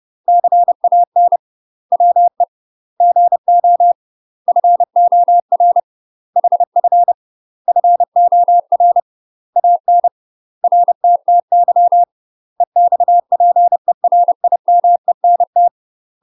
Morse code, heard as a keyed tone, about 30 wpm, 700 Hz. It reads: CAN WE GO FOR HF FOR AN RTTY EXPERIMENT
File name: UK MASTER QSOs TRAINING / 30WPM / UK MASTER QSOs 0024